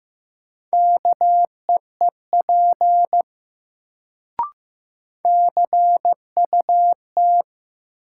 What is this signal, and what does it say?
Morse code, heard as a keyed tone, about 15 wpm, 700 Hz. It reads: KEEP CUT
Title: Sentences from Top 300 Words - New Word Drill - 15wpm